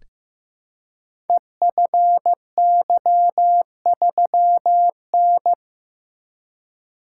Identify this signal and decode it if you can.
Morse code, heard as a keyed tone, about 15 wpm, 700 Hz. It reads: EFY3N
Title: Code Groups - 5 Letter:Numbers - 15wpm